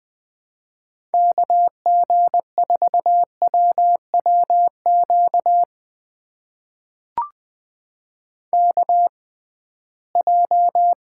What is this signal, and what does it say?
Morse code, heard as a keyed tone, about 20 wpm, spaced as for 15 wpm, 700 Hz. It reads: KG4WWQ KJ
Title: USA Call Signs - 6 Characters - 6x Character Spacing - 20wpm